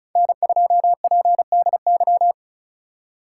Morse code, heard as a keyed tone, about 34 wpm, 700 Hz. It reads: N2PDY
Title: USA Call Signs - 5 Characters - 4x Character Spacing - 35wpm